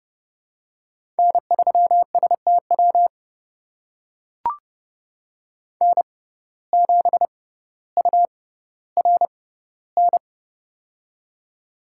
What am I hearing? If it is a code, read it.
Morse code, heard as a keyed tone, about 30 wpm, spaced as for 15 wpm, 700 Hz. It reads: N3STW N7URN
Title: USA Call Signs - 6x Character Spacing - 30wpm